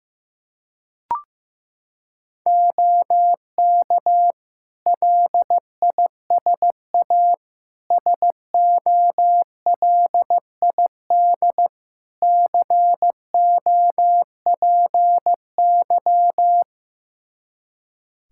Morse code, heard as a keyed tone, about 15 wpm, 700 Hz. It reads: OK LISA SOLID COPY
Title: Practice QSO - Exchange 3 - More Info - 15wpm